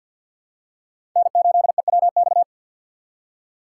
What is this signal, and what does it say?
Morse code, heard as a keyed tone, about 51 wpm, 700 Hz. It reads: N8EWX
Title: K1USN SST QSO Practice - 50wpm